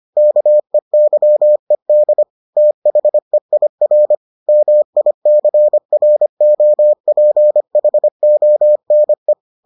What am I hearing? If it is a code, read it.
Morse code, heard as a keyed tone, about 25 wpm, 600 Hz. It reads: KEYED THEIR MICROPHONE